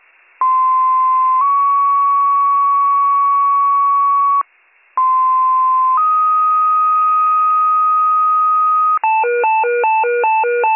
Two-tone page, heard as a one-second tone, then a three-second tone, about 1040 then 1120 Hz, about 1040 then 1250 Hz. It is heard from 0.4 to 9.0 s.